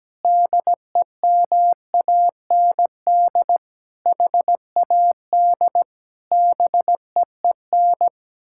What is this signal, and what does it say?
Morse code, heard as a keyed tone, about 17 wpm, 700 Hz. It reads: DEMAND HAD BEEN
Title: Sentences from Top 1000 Words - Review Part 6 - 17wpm